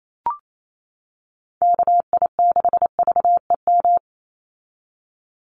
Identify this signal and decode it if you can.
Morse code, heard as a keyed tone, about 28 wpm, 700 Hz. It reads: KI6VEM
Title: USA Call Signs 28wpm